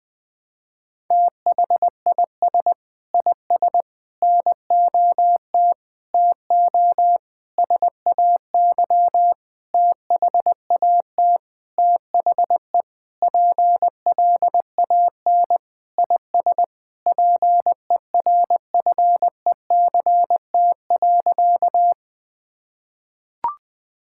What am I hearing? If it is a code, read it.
Morse code, heard as a keyed tone, about 20 wpm, 700 Hz. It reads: THIS IS NOT TO SAY THAT THE PLAN IS PERFECT.
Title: Sentences with 7-Word Phrases - 20wpm